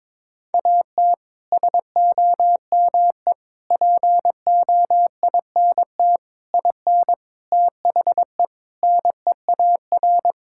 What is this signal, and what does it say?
Morse code, heard as a keyed tone, about 22 wpm, 700 Hz. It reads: AT SOME POINT IN THE NEAR
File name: practice / 22wpm / Sets of 7 Words - 22wpm